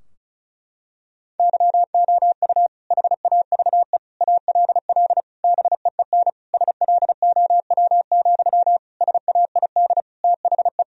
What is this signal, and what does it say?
Morse code, heard as a keyed tone, about 36 wpm, 700 Hz. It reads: YOU HAVE ALL BEEN SLOW, SAID THE
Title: Sentences from Top 1000 Words - Review Part 2 - 35wpm